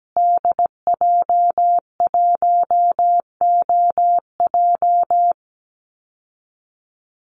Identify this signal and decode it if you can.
Morse code, heard as a keyed tone, about 17 wpm, 700 Hz. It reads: DJ1OJ
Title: Call Signs Repeated 2x - 17wpm